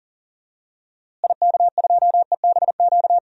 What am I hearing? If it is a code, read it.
Morse code, heard as a keyed tone, about 40 wpm, 700 Hz. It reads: IK2EBQ
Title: DXCC-100 Call Signs - 6 Characters - Repeated 3x - 40wpm